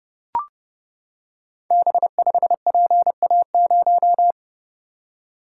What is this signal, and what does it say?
Morse code, heard as a keyed tone, about 30 wpm, 700 Hz. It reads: B5PA0
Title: Code Groups - 5 Letter:Numbers - 30wpm